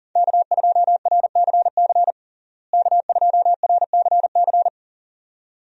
Morse code, heard as a keyed tone, about 37 wpm, 700 Hz. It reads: K2RCC K2RCC
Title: USA Call Signs - 5 Characters - Repeated 2x - 40wpm